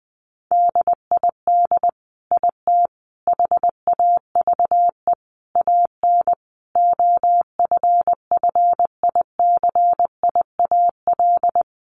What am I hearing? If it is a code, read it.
Morse code, heard as a keyed tone, about 20 wpm, 700 Hz. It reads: DID IT HAVE AN OFFICIAL
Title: Sentences from Top 1200 Words 20wpm